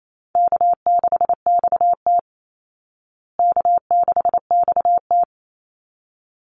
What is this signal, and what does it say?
Morse code, heard as a keyed tone, about 28 wpm, 700 Hz. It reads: K6XT K6XT
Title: Easy Call Signs Repeated 2x - 28wpm